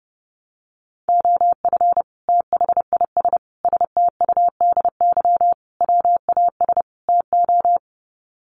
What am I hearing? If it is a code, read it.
Morse code, heard as a keyed tone, about 30 wpm, 700 Hz. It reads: OF THIS STUDY WAS TO
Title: Sets of 5 Words 30wpm